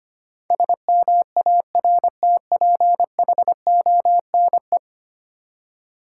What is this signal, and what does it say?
Morse code, heard as a keyed tone, about 25 wpm, 690 Hz. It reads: SMARTPHONE
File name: 10 Letter Words - First 3 Letters Given - 25wpm